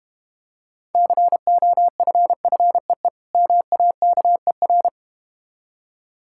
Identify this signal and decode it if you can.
Morse code, heard as a keyed tone, about 32 wpm, 700 Hz. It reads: COFFEE MAKER